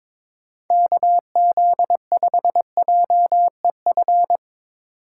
Morse code, heard as a keyed tone, about 21 wpm, 700 Hz. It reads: KZ5JEF
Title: USA Call Signs - 6 Characters - 4x Character Spacing - 22wpm